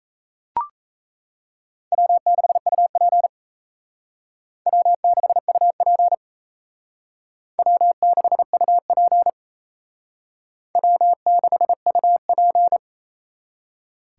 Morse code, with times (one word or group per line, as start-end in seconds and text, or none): 1.9-3.3 W6UP
4.7-6.1 W6UP
7.6-9.3 W6UP
10.8-12.8 W6UP